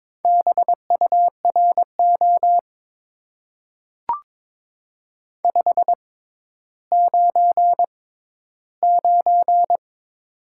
Morse code, heard as a keyed tone, about 22 wpm, spaced as for 13 wpm, 700 Hz. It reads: BURO 599
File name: Top 90 QSO Elements - 6x Character Spacing - 22wpm